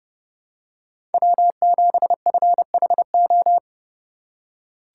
Morse code, heard as a keyed tone, about 30 wpm, 700 Hz. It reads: W7FHO
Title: USA Call Signs - 5 Characters - 4x Character Spacing - 30wpm